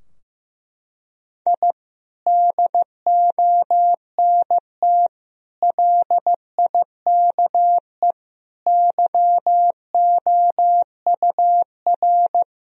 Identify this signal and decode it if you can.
Morse code, heard as a keyed tone, about 15 wpm, 700 Hz. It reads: I DONT LIKE YOUR
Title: Sentences from Top 1000 Words - Review Part 3 - 15wpm